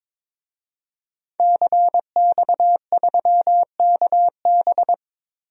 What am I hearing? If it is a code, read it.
Morse code, heard as a keyed tone, about 22 wpm, 700 Hz. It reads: CX3KB
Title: DXCC-100 Call Signs - 5 Characters - 22wpm